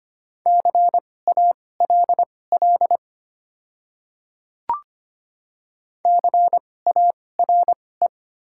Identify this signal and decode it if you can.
Morse code, heard as a keyed tone, about 25 wpm, spaced as for 18 wpm, 690 Hz. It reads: CALL CARE
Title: 2x Character Spacing - Top 300 Words - 25wpm